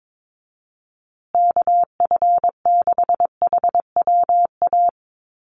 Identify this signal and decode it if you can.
Morse code, heard as a keyed tone, about 22 wpm, 700 Hz. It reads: KF6HWA
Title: USA Call Signs Repeated 2x 22wpm